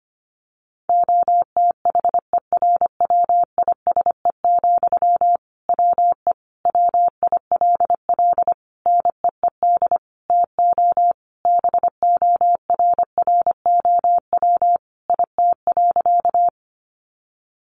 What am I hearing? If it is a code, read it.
Morse code, heard as a keyed tone, about 25 wpm, 700 Hz. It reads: OTHERWISE, WE WILL NEED TO BORROW IT.
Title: Sentences from Top 1500 Words 25wpm